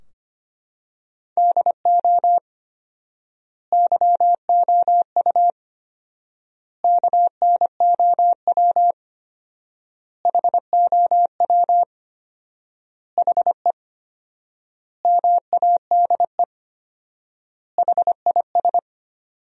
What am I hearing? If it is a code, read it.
Morse code, heard as a keyed tone, about 25 wpm, 700 Hz. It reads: DO YOU KNOW HOW HE MADE HIS
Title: 4x Word-Spacing - Sentences from Top 300 Words - 25wpm